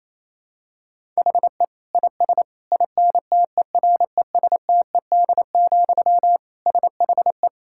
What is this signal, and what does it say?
Morse code, heard as a keyed tone, about 28 wpm, 700 Hz. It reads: HE IS INTERESTED, SHE
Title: Sentences from Top 1000 Words - Review Part 6 - 28wpm